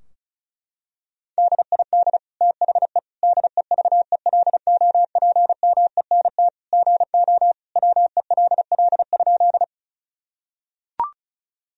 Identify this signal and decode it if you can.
Morse code, heard as a keyed tone, about 34 wpm, 700 Hz. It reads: DID THE DEVELOPMENT GO WELL?
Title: Sentences from Top 1000 Words - Review Part 4 - 35wpm